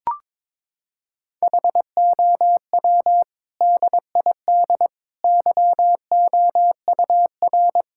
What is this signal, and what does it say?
Morse code, heard as a keyed tone, about 22 wpm, 700 Hz. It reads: HOW DID YOUR
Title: Sentences from Top 1000 Words - Review Part 6 - 22wpm